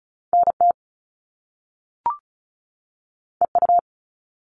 Morse code, heard as a keyed tone, about 36 wpm, 700 Hz. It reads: NT EU
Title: Top Two Letter Combinations in Words 35wpm